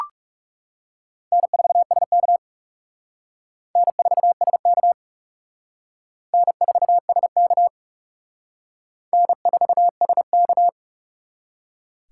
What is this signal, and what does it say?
Morse code, heard as a keyed tone, about 38 wpm, 700 Hz. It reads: N4SK N4SK N4SK N4SK